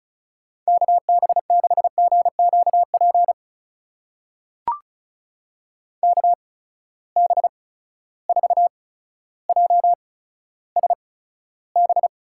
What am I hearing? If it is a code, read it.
Morse code, heard as a keyed tone, about 36 wpm, spaced as for 15 wpm, 700 Hz. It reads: KB6GQP KB4JSB
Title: USA Call Signs - 6 Characters - 8x Character Spacing - 35wpm